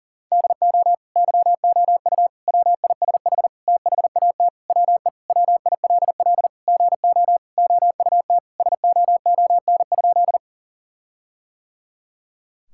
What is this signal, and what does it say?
Morse code, heard as a keyed tone, about 37 wpm, 700 Hz. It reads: DO YOU WISH THAT WE WILL GO OUT SOON?
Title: Sentences from Top 1600 Words 40wpm